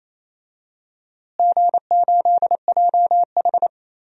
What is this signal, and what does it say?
Morse code, heard as a keyed tone, about 28 wpm, 700 Hz. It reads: G8JH
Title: DXCC-100 Call Signs - 4 Characters - Repeated 3x - 28wpm